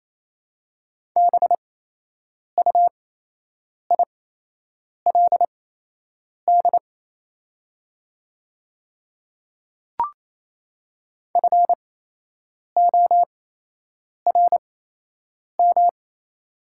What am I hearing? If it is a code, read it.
Morse code, heard as a keyed tone, about 28 wpm, spaced as for 8 wpm, 700 Hz. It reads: BUILD FORM
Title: 8x Character Spacing - Top 200 Words - 28wpm